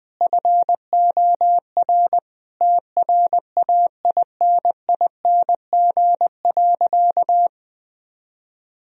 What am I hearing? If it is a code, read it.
Morse code, heard as a keyed tone, about 21 wpm, 700 Hz. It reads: FOR TRAINING.
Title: Sentences from Top 1000 Words - Review Part 4 - 20wpm